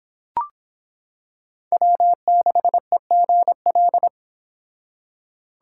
Morse code, heard as a keyed tone, about 25 wpm, 700 Hz. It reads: W6EGL